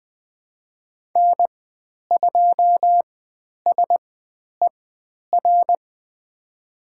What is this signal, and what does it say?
Morse code, heard as a keyed tone, about 20 wpm, spaced as for 10 wpm, 700 Hz. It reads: N2SER
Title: 095 Lesson 16g - Letter c - Call Signs 20wpm